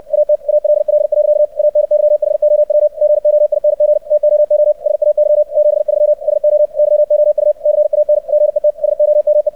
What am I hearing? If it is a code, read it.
Morse code, heard as a keyed tone, about 64 wpm, 600 Hz. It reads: AT ALL. ANYHOW GOING TOO SAY 73 SO YOU CAN GET SOME
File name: archived / CRSnet